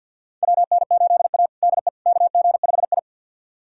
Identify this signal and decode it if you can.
Morse code, heard as a keyed tone, about 49 wpm, 700 Hz. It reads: WN8A DE KG5I
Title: USA Callsign DE Callsign - 4 Characters - 50wpm